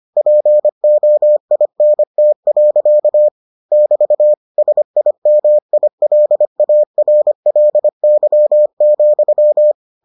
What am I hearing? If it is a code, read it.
Morse code, heard as a keyed tone, about 25 wpm, 600 Hz. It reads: POINT. <BT> SIMILARLY,